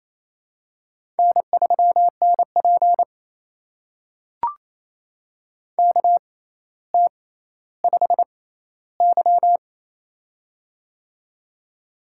Morse code, heard as a keyed tone, about 29 wpm, spaced as for 14 wpm, 700 Hz. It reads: N3NP KT5Y
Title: USA Call Signs - 4 Characters - 6x Character Spacing - 28wpm